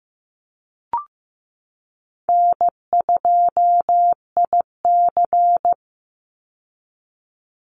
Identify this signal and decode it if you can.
Morse code, heard as a keyed tone, about 15 wpm, 700 Hz. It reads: N2IC
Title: Call Signs Repeated 3x - 15wpm